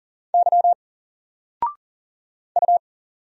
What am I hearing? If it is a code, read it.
Morse code, heard as a keyed tone, about 40 wpm, 700 Hz. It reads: Y U
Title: Single Letter-Number Alt 40wpm